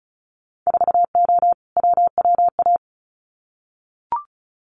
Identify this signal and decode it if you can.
Morse code, heard as a keyed tone, about 36 wpm, 700 Hz. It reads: 4O WWA
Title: Winter Field Day Exchanges 35wpm